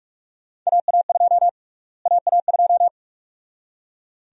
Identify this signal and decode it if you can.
Morse code, heard as a keyed tone, about 46 wpm, 700 Hz. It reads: AA2 AA2
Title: USA Call Sign Prefixes - Repeated 2x - 45wpm